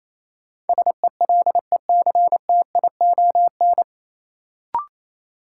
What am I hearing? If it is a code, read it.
Morse code, heard as a keyed tone, about 28 wpm, 700 Hz. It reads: SELECTION
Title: Sentences from Top 1400 Words - New Word Drill - 28wpm